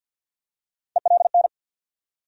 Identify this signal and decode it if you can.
Morse code, heard as a keyed tone, about 47 wpm, 700 Hz. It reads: ELN
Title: USA Call Sign Postfixes - 3 Letters 50wpm